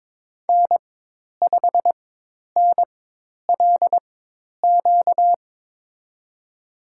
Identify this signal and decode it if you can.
Morse code, heard as a keyed tone, about 22 wpm, spaced as for 10 wpm, 700 Hz. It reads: N5NLQ
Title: USA Call Signs - 5 Characters - 4x Character Spacing - 22wpm